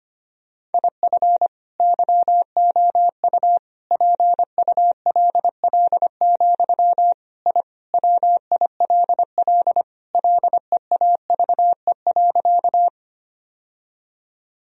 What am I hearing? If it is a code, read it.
Morse code, heard as a keyed tone, about 25 wpm, 700 Hz. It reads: IF YOU PULL, I WILL LEAVE.
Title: Sentences from Top 1000 Words - Review Part 2 - 25wpm